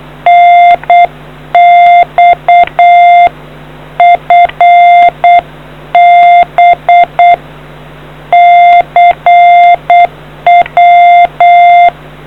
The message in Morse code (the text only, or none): NX FB CW